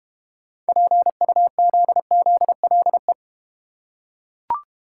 Morse code, heard as a keyed tone, about 30 wpm, 700 Hz. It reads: PUZZLE
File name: Easy Words - Speed-Racing with Hints - 25wpm